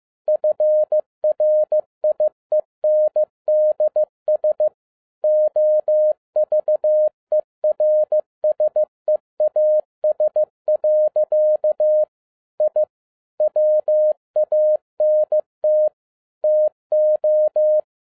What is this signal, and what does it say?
Morse code, heard as a keyed tone, about 15 wpm, 610 Hz. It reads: FRIENDS OVERSEAS. I WANT TO
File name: 2011-07-mp3 / n37 THE 3.11 2 15wpm